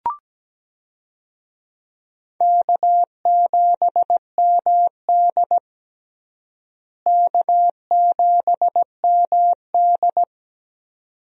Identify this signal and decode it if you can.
Morse code, heard as a keyed tone, about 17 wpm, 700 Hz. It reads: K7MD K7MD
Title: USA Call Signs - 4 Characters - Repeated 2x - 17wpm